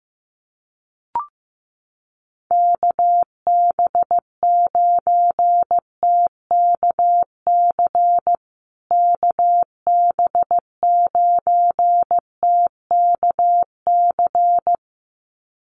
Morse code, heard as a keyed tone, about 15 wpm, 700 Hz. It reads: KB9TKC KB9TKC
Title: USA Call Signs Repeated 2x 15wpm